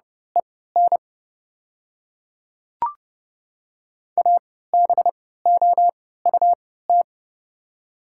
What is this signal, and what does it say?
Morse code, heard as a keyed tone, about 30 wpm, spaced as for 17 wpm, 700 Hz. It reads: EN ABOUT
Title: 3x Character Spacing - Top 200 Words - 30wpm